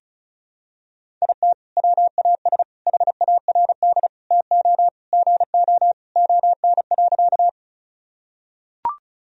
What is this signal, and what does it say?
Morse code, heard as a keyed tone, about 36 wpm, 700 Hz. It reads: IT WAS HARD TO GO ON.
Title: Sentences from Top 1000 Words - Review Part 1 - 35wpm